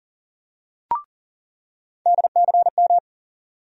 Morse code, heard as a keyed tone, about 40 wpm, 700 Hz. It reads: DCM
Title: USA Call Sign Postfixes - 3 Letters 40wpm